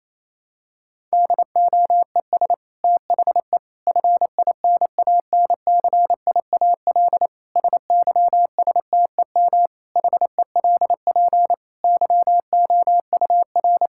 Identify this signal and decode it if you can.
Morse code, heard as a keyed tone, about 28 wpm, 700 Hz. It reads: DOES THE FINANCIAL SYSTEM HELP YOUR
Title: Sentences from Top 1000 Words - Review Part 5 - 28wpm